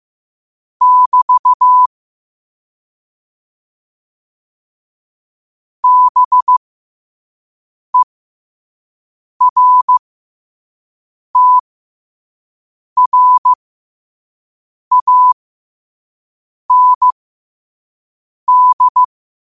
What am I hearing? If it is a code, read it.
Morse code, heard as a keyed tone, about 15 wpm, spaced as for 5 wpm, 1000 Hz. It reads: <BT> BERTRAND
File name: qotdfiles / 5wpm